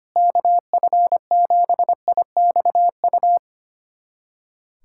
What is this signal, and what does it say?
Morse code, heard as a keyed tone, about 25 wpm, 700 Hz. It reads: KF7IXU
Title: USA Callsign DE Callsign - 6 Characters - 25wpm